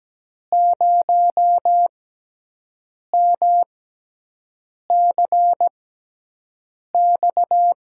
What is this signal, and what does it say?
Morse code, heard as a keyed tone, about 17 wpm, spaced as for 6 wpm, 700 Hz. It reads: 0MCX
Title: USA Call Signs - 5 Characters - 6x Character Spacing - 17wpm